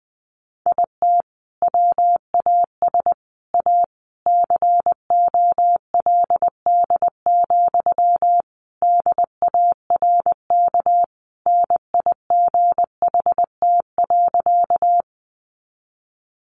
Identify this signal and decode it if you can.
Morse code, heard as a keyed tone, about 20 wpm, 700 Hz. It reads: IT WAS A COLD, DARK NIGHT.
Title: Sentences from Top 500 Words 20wpm